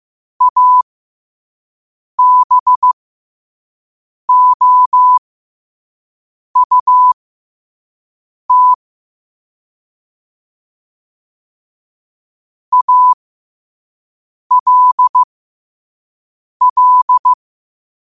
Morse code, heard as a keyed tone, about 15 wpm, spaced as for 5 wpm, 1000 Hz. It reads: ABOUT ALL
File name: qotdfiles / 5wpm